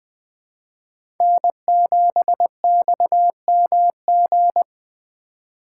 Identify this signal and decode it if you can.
Morse code, heard as a keyed tone, about 20 wpm, 700 Hz. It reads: N7XMG
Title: Speed Racing - USA 1x3 Call Signs 17wpm